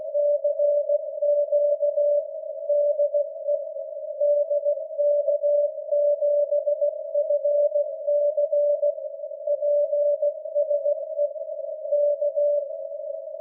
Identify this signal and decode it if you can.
Morse code, heard as a keyed tone, about 16 wpm, 580 Hz. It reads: CQ DE DK7FC PSE K
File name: dk7fc-472kHz